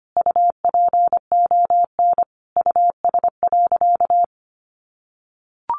Morse code, heard as a keyed tone, about 25 wpm, 700 Hz. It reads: UPON US.
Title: Sentences from Top 1400 Words 25wpm